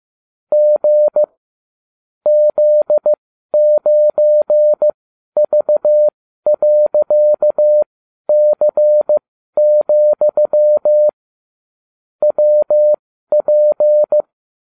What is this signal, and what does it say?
Morse code, heard as a keyed tone, about 15 wpm, spaced as for 12 wpm, 600 Hz. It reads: G Z9V.C, WP